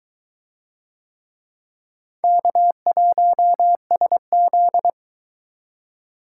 Morse code, heard as a keyed tone, about 22 wpm, 700 Hz. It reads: K1SZ